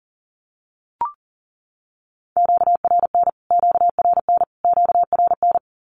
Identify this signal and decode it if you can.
Morse code, heard as a keyed tone, about 40 wpm, 700 Hz. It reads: QRN QRN QRN
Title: Q-Codes Repeated 3x 40wpm